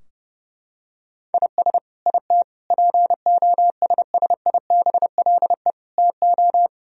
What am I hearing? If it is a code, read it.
Morse code, heard as a keyed tone, about 28 wpm, 700 Hz. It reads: IS IT POSSIBLE TO
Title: Sentences from Top 1000 Words - Review Part 3 - 30wpm